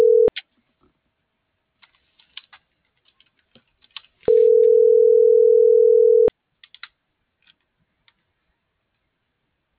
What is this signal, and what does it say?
Ringback tone (North America), 440 + 480 Hz.
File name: CENTRAL MORENA / out ram.lmorena.2025